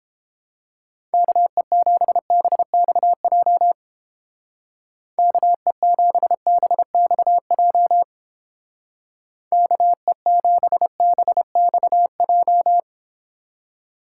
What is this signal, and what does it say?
Morse code, heard as a keyed tone, about 31 wpm, 700 Hz. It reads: KE7BXJ KE7BXJ KE7BXJ